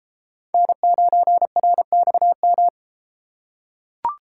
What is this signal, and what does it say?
Morse code, heard as a keyed tone, about 32 wpm, 700 Hz. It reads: N9RXM